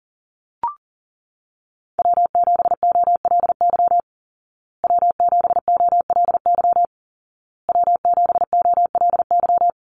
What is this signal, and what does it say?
Morse code, heard as a keyed tone, about 40 wpm, 700 Hz. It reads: W7OLY W7OLY W7OLY
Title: Call Signs Repeated 2x - 40wpm